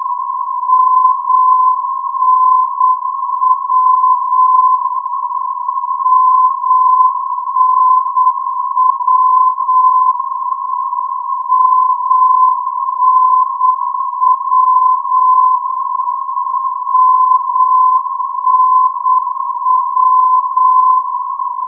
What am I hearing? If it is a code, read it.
Morse code, heard as a keyed tone, about 8 wpm, 1020 Hz. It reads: MNW MNW MNW MNW
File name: europe / d-8524 - ndb muenchen mnw-338 (2005) (rainer brannolte, d) xc